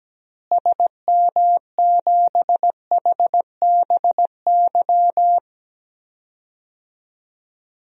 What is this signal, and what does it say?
Morse code, heard as a keyed tone, about 17 wpm, 700 Hz. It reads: SM7HBY
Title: DXCC-100 Call Signs - 6 Characters - Repeated 3x - 17wpm